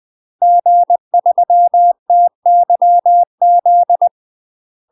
Morse code, heard as a keyed tone, about 20 wpm, 700 Hz. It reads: G3TYZ
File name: UK MASTER QSOs TRAINING / 20WPM / UK MASTER QSOs 0068